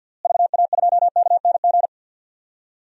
Morse code, heard as a keyed tone, about 47 wpm, 700 Hz. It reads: VA2KNG
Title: DXCC-100 Call Signs - 6 Characters - Repeated 2x - 50wpm